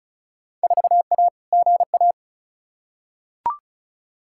Morse code, heard as a keyed tone, about 36 wpm, 700 Hz. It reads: 4A GA
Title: Field Day Exchanges - 35wpm